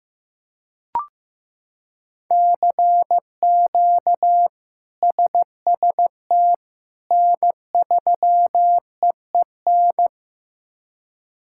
Morse code, heard as a keyed tone, about 15 wpm, 700 Hz. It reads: CQ SST N3EEN
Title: K1USN SST QSO Practice - 15wpm